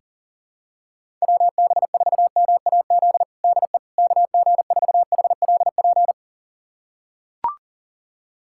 Morse code, heard as a keyed tone, about 40 wpm, 700 Hz. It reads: WB4MAZ DE KG4HLP
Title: USA Callsign DE Callsign - 6 Characters - 40wpm